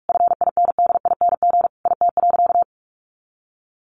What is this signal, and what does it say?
Morse code, heard as a keyed tone, about 44 wpm, 700 Hz. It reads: FINDING IT.
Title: Sentences from Top 1100 Words 45wpm